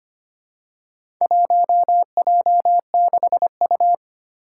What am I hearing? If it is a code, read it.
Morse code, heard as a keyed tone, about 25 wpm, 700 Hz. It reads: 1J6U
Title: Four Challenging Letter-Number Combinations 25wpm